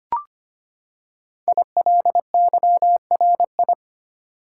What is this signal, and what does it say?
Morse code, heard as a keyed tone, about 25 wpm, 700 Hz. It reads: ILYRI
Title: Code Groups - 5 Letters - 25wpm